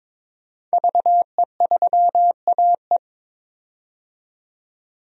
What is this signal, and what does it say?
Morse code, heard as a keyed tone, about 22 wpm, 700 Hz. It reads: VE3AE
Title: DXCC-100 Call Signs - 5 Characters - 22wpm